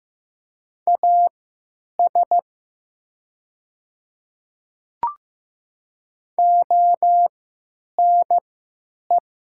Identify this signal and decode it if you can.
Morse code, heard as a keyed tone, about 15 wpm, spaced as for 9 wpm, 700 Hz. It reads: AS ONE
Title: 3x Character Spacing - Top 200 Words - 15wpm